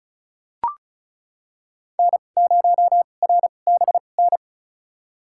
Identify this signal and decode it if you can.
Morse code, heard as a keyed tone, about 35 wpm, spaced as for 25 wpm, 700 Hz. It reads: N0RBN
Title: USA Call Signs - 5 Characters - 2x Character Spacing - 35wpm